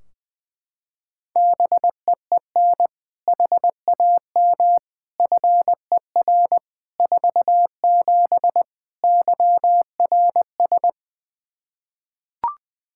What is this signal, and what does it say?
Morse code, heard as a keyed tone, about 21 wpm, 700 Hz. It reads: BEEN HAM FER 47 YRS
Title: Practice QSO - Exchange 3 - More Info - 20wpm